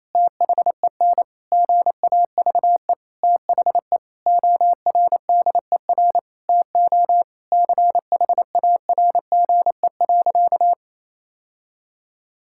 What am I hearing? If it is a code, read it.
Morse code, heard as a keyed tone, about 28 wpm, 700 Hz. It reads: THEN GAVE THE ORDER TO CHARGE.
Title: Sentences from Top 1000 Words - Review Part 6 - 28wpm